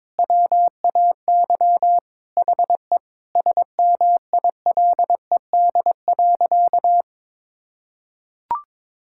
Morse code, heard as a keyed tone, about 22 wpm, 700 Hz. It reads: WAY HE SMILED.
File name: practice / 22wpm / Sentences with 4-Word Phrases - 22wpm